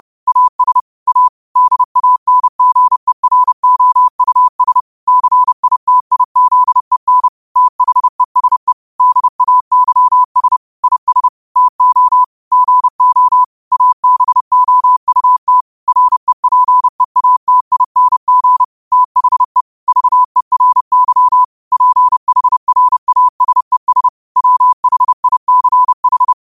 Morse code, heard as a keyed tone, about 30 wpm, 1000 Hz. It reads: AS A DANGEROUS CITIZEN THESE DAYS IS TO GO ABOUT REPEATING THE VERY PHRASES WHICH